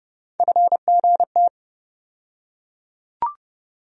Morse code, heard as a keyed tone, about 32 wpm, 700 Hz. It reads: FGT
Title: USA Call Sign Postfixes - 3 Letters 30wpm